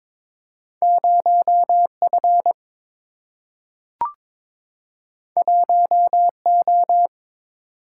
Morse code, heard as a keyed tone, about 22 wpm, 700 Hz. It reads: 0 F 1 O
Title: Single-Letter Single-Number 22wpm